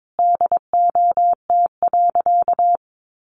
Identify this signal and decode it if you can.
Morse code, heard as a keyed tone, about 22 wpm, 700 Hz. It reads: DOT.
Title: Sentences from Top 1500 Words 22wpm